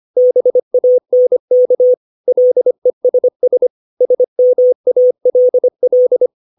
Morse code, heard as a keyed tone, about 25 wpm, 500 Hz. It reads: BANK LESS SMALL
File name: 632 words / 632 Words 25WPM